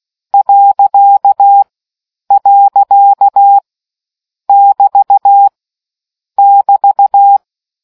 Morse code, heard as a keyed tone, about 16 wpm, 800 Hz. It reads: . <BT> <BT>